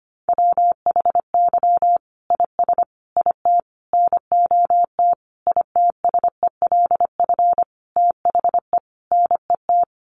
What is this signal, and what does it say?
Morse code, heard as a keyed tone, about 25 wpm, 700 Hz. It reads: WHY IS IT NOT ITSELF THE NET
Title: Sentences from Top 1400 Words 25wpm